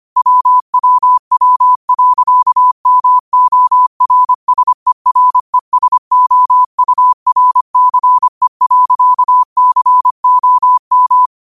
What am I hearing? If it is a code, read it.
Morse code, heard as a keyed tone, about 25 wpm, 1000 Hz. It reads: WWW.MORSERESOURCE.COM